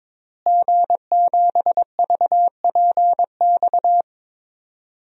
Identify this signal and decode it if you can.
Morse code, heard as a keyed tone, about 22 wpm, 700 Hz. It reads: G7VPX
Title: DXCC-100 Call Signs - 5 Characters - Repeated 3x - 22wpm